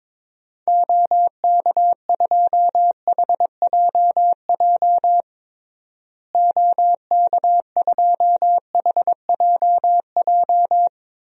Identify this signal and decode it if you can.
Morse code, heard as a keyed tone, about 22 wpm, 700 Hz. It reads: OK2HJJ OK2HJJ
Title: DXCC-100 Call Signs - 6 Characters - Repeated 2x - 22wpm